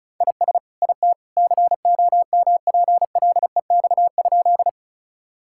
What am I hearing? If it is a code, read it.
Morse code, heard as a keyed tone, about 36 wpm, 700 Hz. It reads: IS IT COMPLEX?
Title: Sentences from Top 1000 Words - Review Part 6 - 35wpm